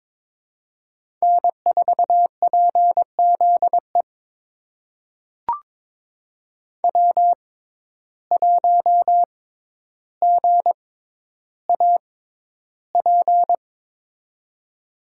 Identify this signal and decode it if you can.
Morse code, heard as a keyed tone, about 21 wpm, spaced as for 11 wpm, 700 Hz. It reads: N4PZE W1GAP
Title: USA Call Signs - 5 Characters - 6x Character Spacing - 22wpm